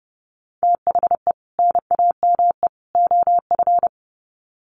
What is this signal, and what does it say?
Morse code, heard as a keyed tone, about 28 wpm, 700 Hz. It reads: THE NAME OF
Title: Sets of 3 Words 30wpm